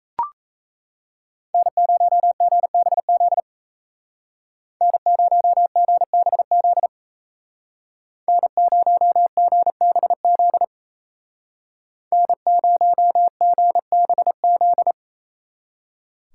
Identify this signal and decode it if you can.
Morse code, heard as a keyed tone, about 33 wpm, 700 Hz. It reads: N0GBZ N0GBZ N0GBZ N0GBZ
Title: Speed Racing - USA 1x3 Call Signs 28wpm